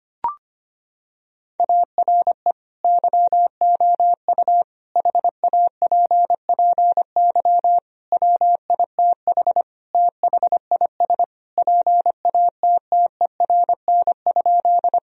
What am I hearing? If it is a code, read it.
Morse code, heard as a keyed tone, about 25 wpm, 700 Hz. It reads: ARE YOU HAPPY WITH THIS PATTERN?
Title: Sentences from Top 1000 Words - Review Part 2 - 25wpm